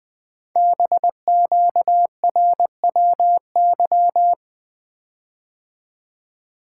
Morse code, heard as a keyed tone, about 20 wpm, 700 Hz. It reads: BQRWY
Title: Code Groups - 5 Letters - 20wpm